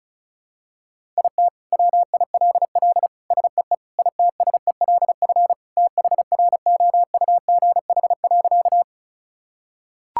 Morse code, heard as a keyed tone, about 34 wpm, 700 Hz. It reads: IT WILL SEE ITSELF THROUGH.
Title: Sentences from Top 1000 Words - Review Part 5 - 35wpm